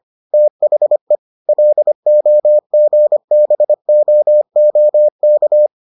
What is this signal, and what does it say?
Morse code, heard as a keyed tone, about 25 wpm, 600 Hz. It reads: THE LOGBOOK